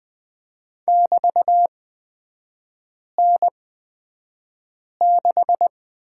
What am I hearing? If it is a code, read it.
Morse code, heard as a keyed tone, about 20 wpm, spaced as for 5 wpm, 700 Hz. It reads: <BT>N6
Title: 243 Lesson 39e - Letter z - Three Characters 20wpm